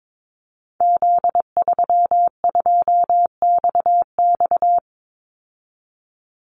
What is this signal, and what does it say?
Morse code, heard as a keyed tone, about 22 wpm, 700 Hz. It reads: Z32XX
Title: Call Signs Repeated 2x - 22wpm